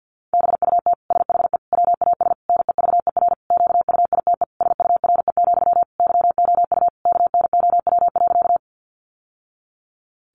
Morse code, heard as a keyed tone, about 49 wpm, 700 Hz. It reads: BUT SHE WAS NEVER QUITE SURE, YOU KNOW.
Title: Sentences from Top 700 Words 50wpm